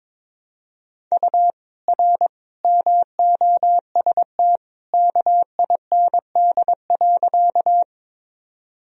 Morse code, heard as a keyed tone, about 21 wpm, 700 Hz. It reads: U R MOST KIND.
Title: Sentences from Top 200 Words and Abbreviations 22wpm